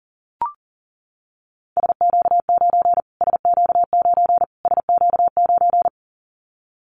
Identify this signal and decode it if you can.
Morse code, heard as a keyed tone, about 40 wpm, 700 Hz. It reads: SQ9 SQ9 SQ9
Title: Callsign Prefixes for Top 25 DXCC - Repeated 3x - 40wpm